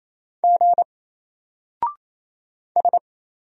Morse code, heard as a keyed tone, about 28 wpm, 700 Hz. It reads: G S
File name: Single-Letter-Number-28wpm